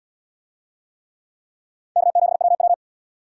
Morse code, heard as a keyed tone, about 77 wpm, 700 Hz. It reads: K7WW